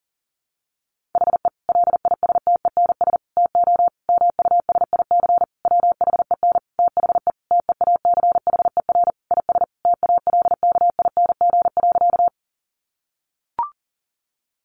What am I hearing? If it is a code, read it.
Morse code, heard as a keyed tone, about 40 wpm, 700 Hz. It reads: HE LISTENS TO MUSIC WHEN THE TEACHER IS TALKING.
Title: Additional Easy Phrases 40wpm